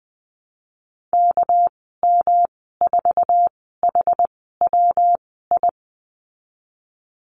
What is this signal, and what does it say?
Morse code, heard as a keyed tone, about 20 wpm, spaced as for 15 wpm, 700 Hz. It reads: KM4HWI
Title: 213 Lesson 34g - Number 8 - Call Signs 20wpm